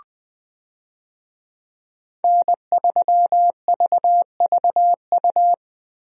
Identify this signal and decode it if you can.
Morse code, heard as a keyed tone, about 20 wpm, 700 Hz. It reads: N3VVU
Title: USA Call Signs - 5 Characters - Repeated 3x - 20wpm